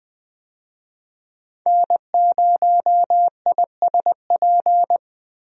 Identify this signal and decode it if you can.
Morse code, heard as a keyed tone, about 20 wpm, 700 Hz. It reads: N0ISP
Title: USA Call Signs - 5 Characters - Repeated 3x - 20wpm